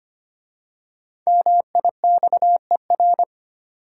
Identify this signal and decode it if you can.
Morse code, heard as a keyed tone, about 25 wpm, 690 Hz. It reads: MIXER
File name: practice / 25wpm / Easy Words with Hints - 25wpm